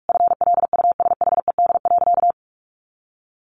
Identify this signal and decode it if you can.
Morse code, heard as a keyed tone, about 44 wpm, 700 Hz. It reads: FLUSHED.
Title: A Sound Of Thunder 45wpm